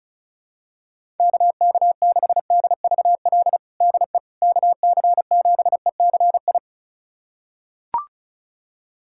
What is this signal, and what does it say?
Morse code, heard as a keyed tone, about 36 wpm, 700 Hz. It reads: KK6DVL DE KC7ECI
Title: USA Callsign DE Callsign - 6 Characters - 35wpm